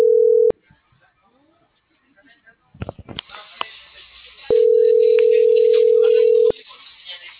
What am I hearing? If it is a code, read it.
Ringback tone (North America), 440 + 480 Hz.